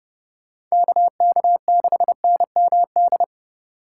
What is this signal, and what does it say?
Morse code, heard as a keyed tone, about 37 wpm, 700 Hz. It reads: KK6NMD
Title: Speed Racing - USA Call Signs 30wpm